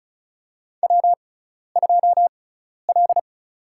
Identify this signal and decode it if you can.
Morse code, heard as a keyed tone, about 36 wpm, spaced as for 12 wpm, 700 Hz. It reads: W2L
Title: USA Call Signs - 5 Characters - 6x Character Spacing - 35wpm